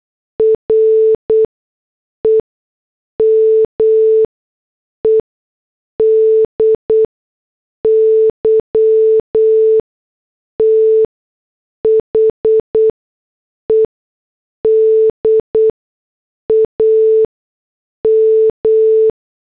Morse code, heard as a keyed tone, about 8 wpm, spaced as for 6 wpm, 440 Hz. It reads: REMEDY THE DAM
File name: morse / headlines